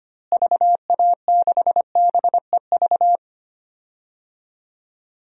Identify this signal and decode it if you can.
Morse code, heard as a keyed tone, about 25 wpm, 700 Hz. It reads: VA6BEV
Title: DXCC-100 Call Signs - 6 Characters - Repeated 3x - 25wpm